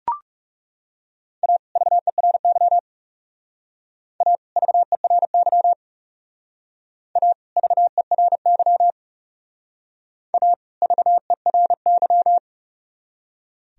Morse code, heard as a keyed tone, about 37 wpm, 700 Hz. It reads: A VERY A VERY A VERY A VERY
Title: Speed-Racing - Sets of 2 Words 30wpm